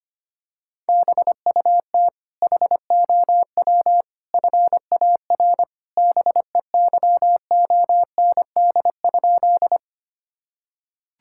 Morse code, heard as a keyed tone, about 25 wpm, 700 Hz. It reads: BUT HOW FAR BEYOND?
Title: Sentences from Top 1000 Words - Review Part 6 - 25wpm